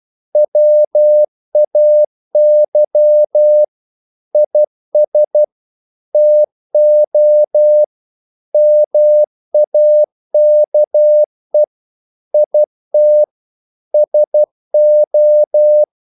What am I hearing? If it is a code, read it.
Morse code, heard as a keyed tone, about 12 wpm, 600 Hz. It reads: WAY IS TO MAKE IT SO
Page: English quotations in CW (telegraphy - morse)